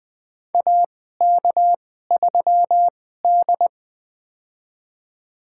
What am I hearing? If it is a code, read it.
Morse code, heard as a keyed tone, about 20 wpm, spaced as for 15 wpm, 700 Hz. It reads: AK3D
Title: USA Call Signs - 4 Characters - 2x Character Spacing - 20wpm